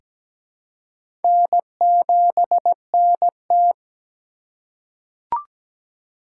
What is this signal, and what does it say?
Morse code, heard as a keyed tone, about 17 wpm, 700 Hz. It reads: N7NT
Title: K1USN SST QSO Practice - 17wpm